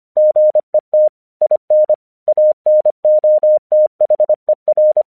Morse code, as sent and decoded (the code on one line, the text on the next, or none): --. - / .. -. / .- -. --- - .... .-.
GET IN ANOTHER